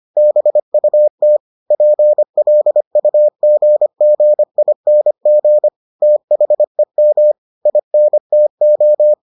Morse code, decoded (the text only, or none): BUT PLUGGING THEM INTO